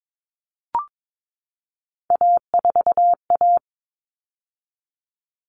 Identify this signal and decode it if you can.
Morse code, heard as a keyed tone, about 22 wpm, 700 Hz. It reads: A4A
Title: All Cut Numbers - 3-Digit 22wpm